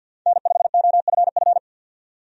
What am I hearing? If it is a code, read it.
Morse code, heard as a keyed tone, about 51 wpm, 700 Hz. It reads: N5OFF
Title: USA Call Signs - 5 Characters - 50wpm